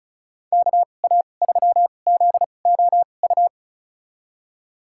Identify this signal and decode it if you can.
Morse code, heard as a keyed tone, about 36 wpm, spaced as for 25 wpm, 700 Hz. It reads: KA3ZOU
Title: USA Call Signs - 6 Characters - 2x Character Spacing - 35wpm